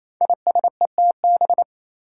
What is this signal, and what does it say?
Morse code, heard as a keyed tone, about 27 wpm, 700 Hz. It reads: ISETB
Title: Five Challenging Letter-Number Combinations 28wpm